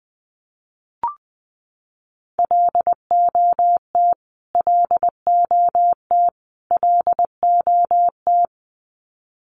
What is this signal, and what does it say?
Morse code, heard as a keyed tone, about 20 wpm, 700 Hz. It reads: LOT LOT LOT
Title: Top 500 Words Repeated 20wpm